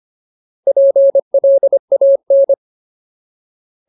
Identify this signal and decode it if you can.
Morse code, heard as a keyed tone, about 25 wpm, 550 Hz. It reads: PLAN
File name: Ultimate Word List / 25WPM Ultimate